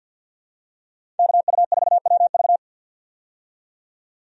Morse code, heard as a keyed tone, about 50 wpm, 700 Hz. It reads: KU4WV
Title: USA Call Signs - 5 Characters - 50wpm